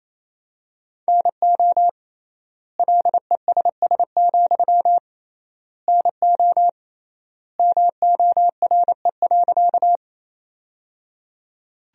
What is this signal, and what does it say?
Morse code, heard as a keyed tone, about 28 wpm, 700 Hz. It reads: NO LESS, NO MORE.
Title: 3x Word-Spacing - Sentences from Top 400 Words - 28wpm